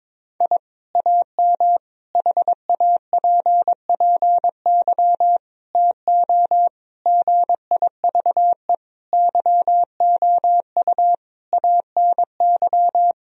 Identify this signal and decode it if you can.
Morse code, heard as a keyed tone, about 22 wpm, 700 Hz. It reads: I AM HAPPY TO GIVE YOU ANY
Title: Sentences from Top 1000 Words - Review Part 3 - 22wpm